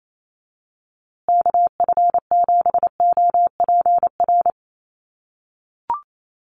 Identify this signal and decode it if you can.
Morse code, heard as a keyed tone, about 28 wpm, 700 Hz. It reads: KF7OPR
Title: USA Call Signs Repeated 2x 28wpm